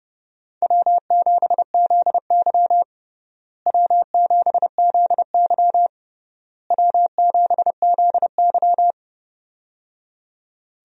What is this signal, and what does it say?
Morse code, heard as a keyed tone, about 30 wpm, 700 Hz. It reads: W7ZY W7ZY W7ZY
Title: USA Call Signs - 4 Characters - Repeated 3x - 30wpm